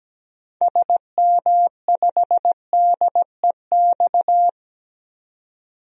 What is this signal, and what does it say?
Morse code, heard as a keyed tone, about 17 wpm, 700 Hz. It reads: SM5DEX
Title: DXCC-100 Call Signs - 6 Characters - 17wpm